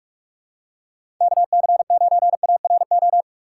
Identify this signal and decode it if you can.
Morse code, heard as a keyed tone, about 46 wpm, 700 Hz. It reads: KC9ARO
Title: USA Call Signs - 6 Characters - 2x Character Spacing - 45wpm